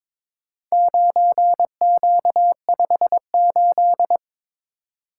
Morse code, heard as a keyed tone, about 21 wpm, 700 Hz. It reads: 9Q58
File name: practice / 22wpm / Four Letter-Number Combinations - 22wpm